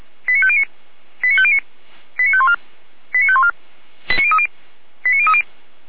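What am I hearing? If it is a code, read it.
Five-tone selective call (ZVEI), heard as quick runs of five tones, tones of about 70 ms. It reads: 89509 89509 89514 89514 89030 89030
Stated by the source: ZVEI five-tone selective call 89042 received by SDR.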